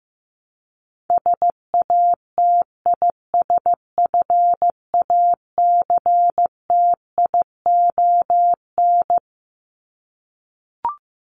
Morse code, heard as a keyed tone, about 15 wpm, 700 Hz. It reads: SATISFACTION
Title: Top 1751-2000 Words 15wpm